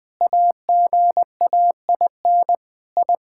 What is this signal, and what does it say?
Morse code, heard as a keyed tone, about 20 wpm, 700 Hz. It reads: AGAIN I
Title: Sentences with 9-Word Phrases - 20wpm